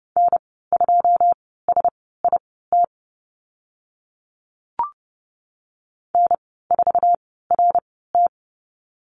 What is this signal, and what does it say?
Morse code, heard as a keyed tone, about 30 wpm, spaced as for 17 wpm, 700 Hz. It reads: N2SIT N4RT
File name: Lesson 14g - Number 2 - Call Signs 30wpm